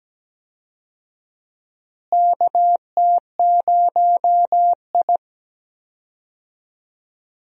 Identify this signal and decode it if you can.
Morse code, heard as a keyed tone, about 17 wpm, 700 Hz. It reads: KT0I
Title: USA Call Signs - 4 Characters - Repeated 2x - 17wpm